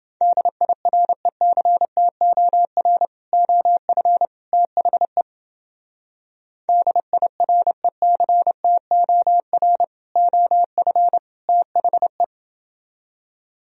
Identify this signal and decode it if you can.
Morse code, heard as a keyed tone, about 28 wpm, 700 Hz. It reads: DIRECTOR OF THE DIRECTOR OF THE